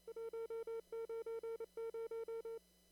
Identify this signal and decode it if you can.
Morse code, heard as a keyed tone, about 29 wpm, 460 Hz. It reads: 190
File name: CW / kt5j wpx cw-op-k5pi